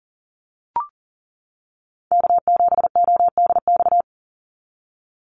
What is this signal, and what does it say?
Morse code, heard as a keyed tone, about 40 wpm, 700 Hz. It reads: K7ODX
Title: NA SOTA Activator Callsigns 40wpm